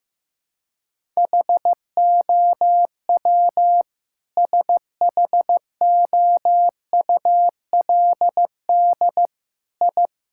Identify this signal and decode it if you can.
Morse code, heard as a keyed tone, about 15 wpm, 700 Hz. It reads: HOW SHOULD I
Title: Sentences from Top 1000 Words - Review Part 5 - 15wpm